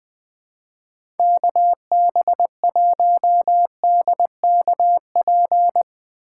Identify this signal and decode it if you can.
Morse code, heard as a keyed tone, about 20 wpm, 700 Hz. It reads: KB1DKP
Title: USA Call Signs - 6 Characters - 2x Character Spacing - 20wpm